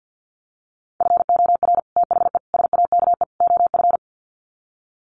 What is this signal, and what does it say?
Morse code, heard as a keyed tone, about 50 wpm, 700 Hz. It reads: FOR THE SAKE OF
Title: Sets of 4 Words 50wpm